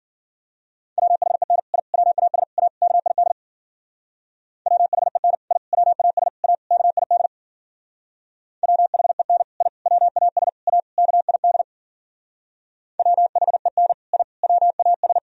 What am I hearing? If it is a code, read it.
Morse code, heard as a keyed tone, about 51 wpm, 700 Hz. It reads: WHEN I WAS A KID WHEN I WAS A KID WHEN I WAS A KID WHEN I WAS